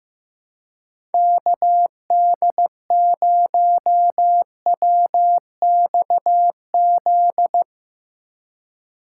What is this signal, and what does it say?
Morse code, heard as a keyed tone, about 15 wpm, 700 Hz. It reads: KD0WXZ
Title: USA Call Signs - 6 Characters - 6x Character Spacing - 15wpm